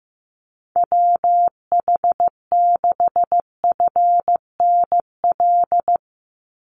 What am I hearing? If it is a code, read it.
Morse code, heard as a keyed tone, about 15 wpm, 700 Hz. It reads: WH6FNL
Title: USA Call Signs 15wpm